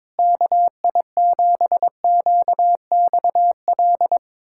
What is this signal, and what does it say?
Morse code, heard as a keyed tone, about 22 wpm, 700 Hz. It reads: KI7QXL
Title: USA Callsign DE Callsign - 6 Characters - 22wpm